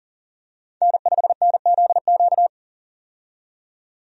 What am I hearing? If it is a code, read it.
Morse code, heard as a keyed tone, about 40 wpm, 700 Hz. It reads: N5NZQ
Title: USA Call Signs - 5 Characters - 40wpm